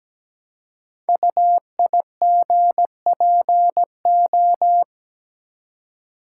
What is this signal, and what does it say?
Morse code, heard as a keyed tone, about 17 wpm, 700 Hz. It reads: UIGPO
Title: Code Groups - 5 Letters - 17wpm